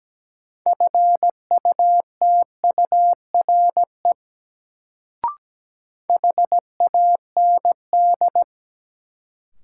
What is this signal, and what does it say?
Morse code, heard as a keyed tone, about 17 wpm, 700 Hz. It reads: FUTURE HAND
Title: Sets of 3 Words - New Word Drill - 17wpm